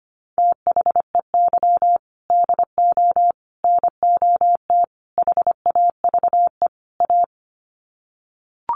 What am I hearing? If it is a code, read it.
Morse code, heard as a keyed tone, about 25 wpm, 700 Hz. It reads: THEY DO NOT HAVE A
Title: Sets of 5 Words 25wpm